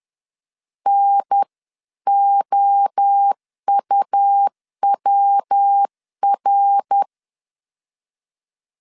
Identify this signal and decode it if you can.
Morse code, heard as a keyed tone, about 10 wpm, 800 Hz. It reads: NOUWR
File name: 10wpm